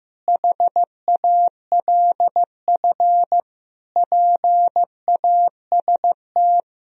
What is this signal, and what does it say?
Morse code, heard as a keyed tone, about 15 wpm, 700 Hz. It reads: HALF PAST
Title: Sentences from Top 1000 Words - Review Part 3 - 15wpm